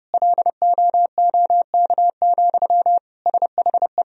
Morse code, heard as a keyed tone, about 32 wpm, 700 Hz. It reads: LOOK, SHE
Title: Sentences from Top 1000 Words - Review Part 1 - 30wpm